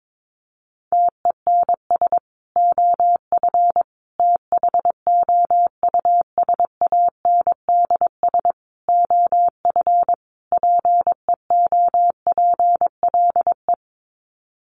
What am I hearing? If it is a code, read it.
Morse code, heard as a keyed tone, about 22 wpm, 700 Hz. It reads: TENS OF THOUSANDS OF PEOPLE
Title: Sets of 5 Words Encore2 22wpm